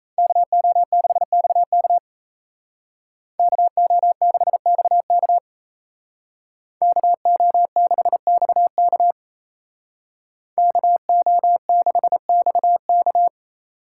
Morse code, heard as a keyed tone, about 36 wpm, 700 Hz. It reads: KO6XK KO6XK KO6XK KO6XK